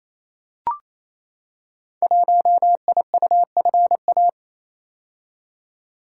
Morse code, heard as a keyed tone, about 29 wpm, 700 Hz. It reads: 1IUFA
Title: Code Groups - 5 Letter:Numbers - 28wpm